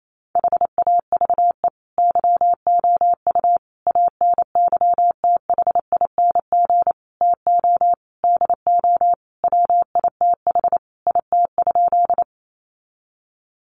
Morse code, heard as a keyed tone, about 28 wpm, 700 Hz. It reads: HAVE YOU ANYTHING TO DO WITH IT?
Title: Sentences from Top 1000 Words 28wpm